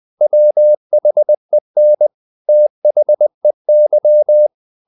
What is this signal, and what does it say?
Morse code, heard as a keyed tone, about 20 wpm, 600 Hz. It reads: WHEN THEY
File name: the old man and the sea / the old man and the sea part 1 20wpm